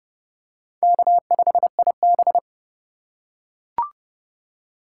Morse code, heard as a keyed tone, about 30 wpm, 700 Hz. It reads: K5IB
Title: USA Call Signs - 4 Characters - 4x Character Spacing - 30wpm